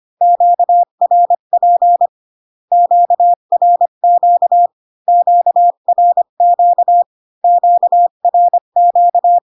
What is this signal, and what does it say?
Morse code, heard as a keyed tone, about 25 wpm, 700 Hz. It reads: QRP QRQ QRQ QRQ